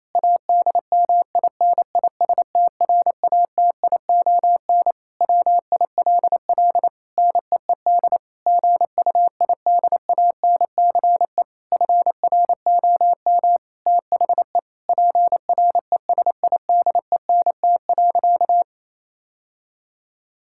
Morse code, heard as a keyed tone, about 28 wpm, 700 Hz. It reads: ADMINISTRATION WILL NEED GUIDANCE FROM THE PRESIDENT.
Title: Sentences from Top 1600 Words 28wpm